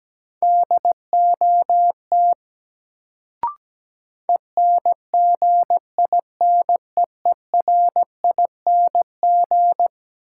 Morse code, heard as a keyed tone, about 17 wpm, 700 Hz. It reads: DOT ENGINEERING
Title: Sentences from Top 1500 Words - New Word Drill Alt - 17wpm